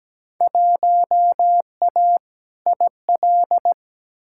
Morse code, heard as a keyed tone, about 17 wpm, 700 Hz. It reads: 1A IL
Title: Field Day Exchanges - 17wpm